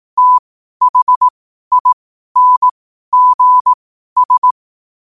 Morse code, heard as a keyed tone, about 18 wpm, spaced as for 12 wpm, 1000 Hz. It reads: THINGS
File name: qotdfiles / 12wpm